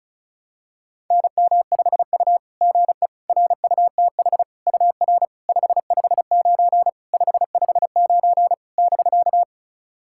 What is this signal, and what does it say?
Morse code, heard as a keyed tone, about 36 wpm, 700 Hz. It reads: NM5U GE RUTH UR 559 559 <BK>
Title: SOTA QSO Practice 35wpm